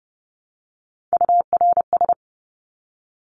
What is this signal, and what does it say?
Morse code, heard as a keyed tone, about 30 wpm, 700 Hz. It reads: URS
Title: Abbreviations 30wpm